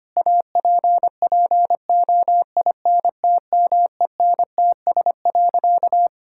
Morse code, heard as a keyed tone, about 25 wpm, 700 Hz. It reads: APPOINTMENTS.
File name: practice / 25wpm / Difficult Sentences with Hints - 25wpm